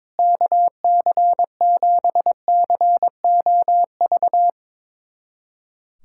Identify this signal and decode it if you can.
Morse code, heard as a keyed tone, about 22 wpm, 700 Hz. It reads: KC7COV
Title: USA Callsign DE Callsign 22wpm